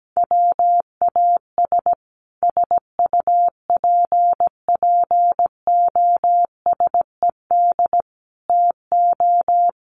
Morse code, heard as a keyed tone, about 17 wpm, 700 Hz. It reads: WAS SUPPOSED TO
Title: Sets of 5 Words Encore 17wpm